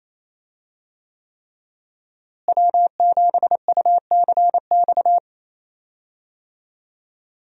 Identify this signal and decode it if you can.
Morse code, heard as a keyed tone, about 28 wpm, 700 Hz. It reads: W7UCX
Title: USA Call Signs - 5 Characters - Repeated 3x - 28wpm